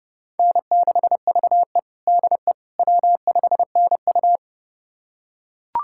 Morse code, heard as a keyed tone, about 30 wpm, 700 Hz. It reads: N6VE DE W5NU
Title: USA Callsign DE Callsign - 4 Characters - 30wpm